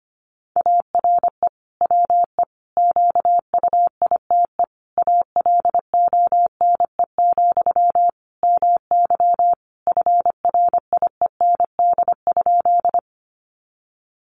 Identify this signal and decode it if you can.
Morse code, heard as a keyed tone, about 25 wpm, 700 Hz. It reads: ARE WE QUITE ALONE, MY FRIEND?
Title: Sentences from Top 1000 Words 25wpm